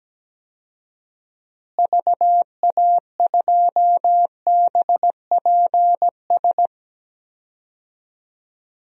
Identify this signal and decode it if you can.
Morse code, heard as a keyed tone, about 17 wpm, 700 Hz. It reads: VA2BPS
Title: DXCC-100 Call Signs - 6 Characters - Repeated 2x - 17wpm